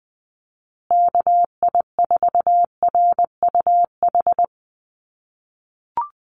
Morse code, heard as a keyed tone, about 20 wpm, 700 Hz. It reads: KI4RUH
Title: USA Call Signs Repeated 2x 20wpm